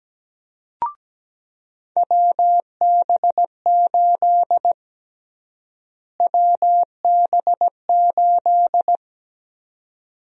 Morse code, heard as a keyed tone, about 17 wpm, 700 Hz. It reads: WB8 WB8
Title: USA Call Sign Prefixes - Repeated 2x - 17wpm